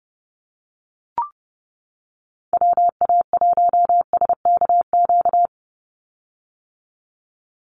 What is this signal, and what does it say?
Morse code, heard as a keyed tone, about 30 wpm, 700 Hz. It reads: WA1SKQ
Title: Call Signs - 30wpm